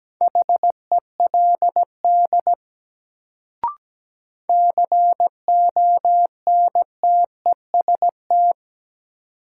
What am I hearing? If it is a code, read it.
Morse code, heard as a keyed tone, about 17 wpm, 700 Hz. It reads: HELD CONTEST
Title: Sentences from Top 1600 Words - New Word Drill - 17wpm